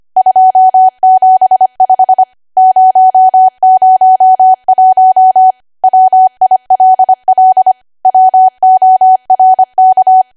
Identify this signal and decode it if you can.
Morse code, heard as a keyed tone, about 25 wpm, 750 Hz. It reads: 275 001 WILL WORK